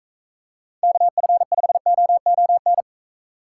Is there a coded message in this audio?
Yes: Morse code from 0.8 to 2.8 s, heard as a keyed tone, about 38 wpm, 700 Hz. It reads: KF5OON